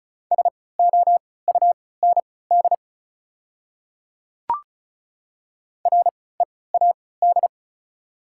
Morse code, heard as a keyed tone, about 34 wpm, spaced as for 20 wpm, 700 Hz. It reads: SOUND READ
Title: 3x Character Spacing - Top 200 Words - 35wpm